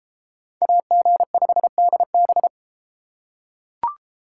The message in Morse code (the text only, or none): AG5DB